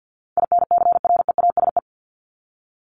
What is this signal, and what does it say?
Morse code, heard as a keyed tone, about 48 wpm, 700 Hz. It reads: INCREASE
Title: LY Suffix - Top 100 Words with Base Words 50wpm